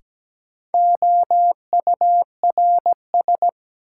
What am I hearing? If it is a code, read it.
Morse code, heard as a keyed tone, about 17 wpm, 700 Hz. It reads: OURS
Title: Sentences from Top 1000 Words - Review Part 5 - New Word Drill - 17wpm